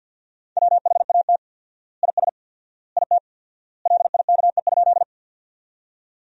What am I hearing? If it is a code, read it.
Morse code, heard as a keyed tone, about 50 wpm, 700 Hz. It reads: WHAT IS IT LIKE?
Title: 4x Word-Spacing - Sentences from Top 100 Words - 50wpm